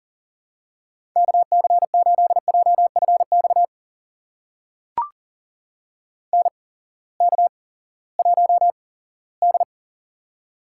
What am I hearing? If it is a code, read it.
Morse code, heard as a keyed tone, about 37 wpm, spaced as for 19 wpm, 700 Hz. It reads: KC8JFX NK1D
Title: USA Call Signs - 8x Character Spacing - 40wpm